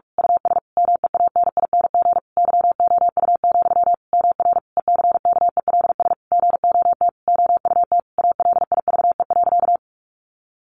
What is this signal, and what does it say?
Morse code, heard as a keyed tone, about 44 wpm, 700 Hz. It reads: US MEANING YOU, MR ECKELS GOT OUT ALIVE.
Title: A Sound Of Thunder 45wpm